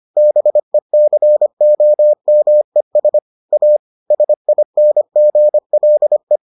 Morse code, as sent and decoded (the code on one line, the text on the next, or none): -... -.-. --- -- . ... / .- / ... .. -. --. .-.. .
BECOMES A SINGLE